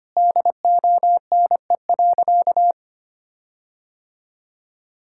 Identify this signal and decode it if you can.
Morse code, heard as a keyed tone, about 25 wpm, 700 Hz. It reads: DONE.
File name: practice / 25wpm / Sentences with 9-Word Phrases - 25wpm